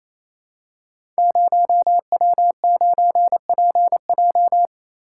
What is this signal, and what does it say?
Morse code, heard as a keyed tone, about 28 wpm, 700 Hz. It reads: 0W9PJ
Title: Code Groups - 5 Letter:Numbers - 28wpm